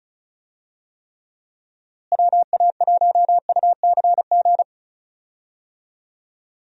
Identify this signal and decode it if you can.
Morse code, heard as a keyed tone, about 36 wpm, 700 Hz. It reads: WA1UCG
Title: USA Call Signs - 6 Characters - Repeated 2x - 35wpm